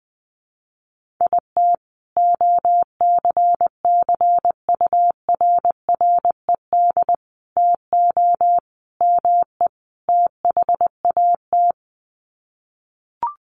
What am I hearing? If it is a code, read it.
Morse code, heard as a keyed tone, about 20 wpm, 700 Hz. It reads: IT OCCURRED TO ME THAT
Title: Sets of 5 Words Encore 20wpm